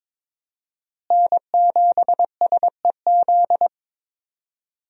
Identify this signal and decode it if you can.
Morse code, heard as a keyed tone, about 22 wpm, 700 Hz. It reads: N7SEZ
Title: USA Call Signs - 5 Characters - Repeated 2x - 22wpm